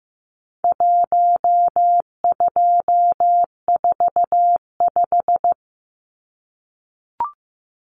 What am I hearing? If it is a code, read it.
Morse code, heard as a keyed tone, about 15 wpm, 700 Hz. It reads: 1245
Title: Numbers - Contest Style - 4-Digit 15wpm